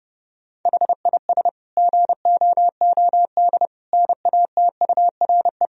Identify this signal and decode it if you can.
Morse code, heard as a keyed tone, about 30 wpm, 700 Hz. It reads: HIS GOOD NATURE
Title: Sentences from Top 1000 Words - Review Part 1 - 30wpm